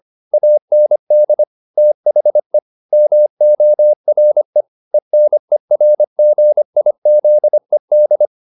Morse code, heard as a keyed tone, about 25 wpm, 600 Hz. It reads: AND THE MORE ENERGIZED